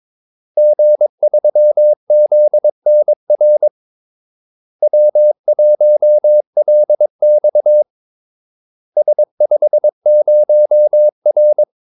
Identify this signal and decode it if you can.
Morse code, heard as a keyed tone, about 22 wpm, 600 Hz. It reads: G3ZNR W1LX S50R
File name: Callsigns-22wpm